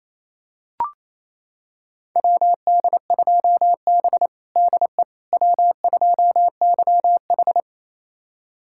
Morse code, heard as a keyed tone, about 28 wpm, 690 Hz. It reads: WD2B DE W2YH
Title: USA Callsign DE Callsign - 4 Characters - 28wpm